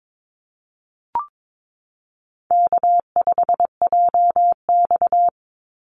Morse code, heard as a keyed tone, about 22 wpm, 700 Hz. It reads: K5JX
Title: Call Signs - 22wpm